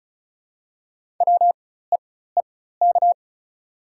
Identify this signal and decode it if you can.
Morse code, heard as a keyed tone, about 34 wpm, spaced as for 16 wpm, 690 Hz. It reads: WEEK
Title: 4x Character Spacing - Top 500 Words - 35wpm